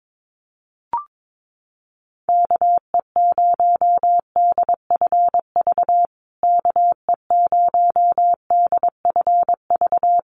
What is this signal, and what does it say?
Morse code, heard as a keyed tone, about 22 wpm, 700 Hz. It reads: KE0DFV KE0DFV
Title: USA Call Signs Repeated 3x 22wpm